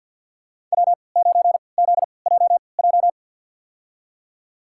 Morse code, heard as a keyed tone, about 47 wpm, spaced as for 28 wpm, 700 Hz. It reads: W9ZJJ
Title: USA Call Signs - 5 Characters - 3x Character Spacing - 50wpm